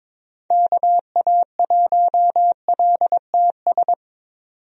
Morse code, heard as a keyed tone, about 23 wpm, 700 Hz. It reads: KA1LTS
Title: USA Call Signs - 6 Characters - 22wpm